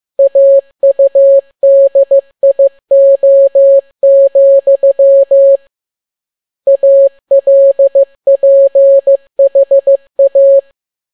Morse code, heard as a keyed tone, about 15 wpm, 550 Hz. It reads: AUDIO, ALPHA